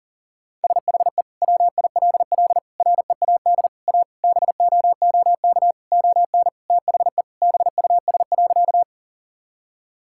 Morse code, heard as a keyed tone, about 40 wpm, 700 Hz. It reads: SHE WILL READ A BOOK ON THE BUS.
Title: Sentences from Top 1000 Words - Review Part 6 - 40wpm